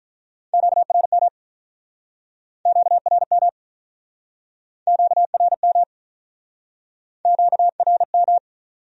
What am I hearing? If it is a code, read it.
Morse code, heard as a keyed tone, about 49 wpm, 700 Hz. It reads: QRM QRM QRM QRM